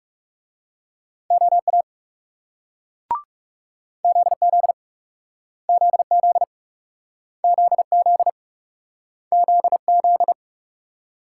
Morse code, heard as a keyed tone, about 42 wpm, 700 Hz. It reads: OA ZZ ZZ ZZ ZZ